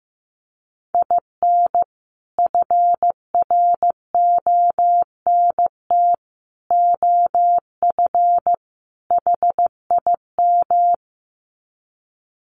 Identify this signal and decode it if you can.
Morse code, heard as a keyed tone, about 15 wpm, 700 Hz. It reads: IN FRONT OF HIM
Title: Sets of 4 Words 15wpm